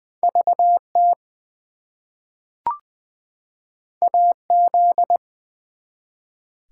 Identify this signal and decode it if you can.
Morse code, heard as a keyed tone, about 20 wpm, 700 Hz. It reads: VT AZ
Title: US States and Canadian Provinces 20wpm